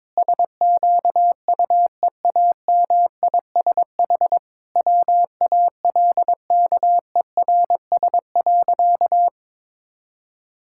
Morse code, heard as a keyed tone, about 22 wpm, 700 Hz. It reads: SQUEAMISH WALKERS.
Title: Pangram Challenge v2 - 22wpm